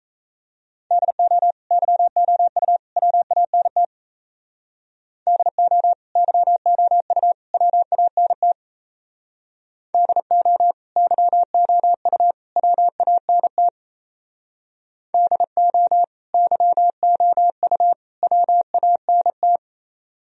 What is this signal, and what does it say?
Morse code, heard as a keyed tone, about 33 wpm, 700 Hz. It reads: DO YOU WANT DO YOU WANT DO YOU WANT DO YOU WANT
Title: Speed-Racing - Sets of 3 Words 28wpm